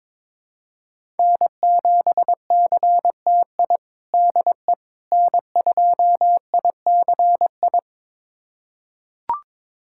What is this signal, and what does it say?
Morse code, heard as a keyed tone, about 22 wpm, 700 Hz. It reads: N7CTI DE N2ICI
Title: USA Callsign DE Callsign - 5 Characters - 22wpm